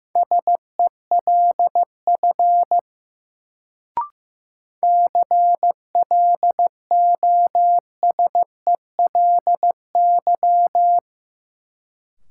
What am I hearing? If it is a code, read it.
Morse code, heard as a keyed tone, about 15 wpm, 700 Hz. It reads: SELF CLOSELY
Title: Sentences from Top 1400 Words - New Word Drill Alt - 15wpm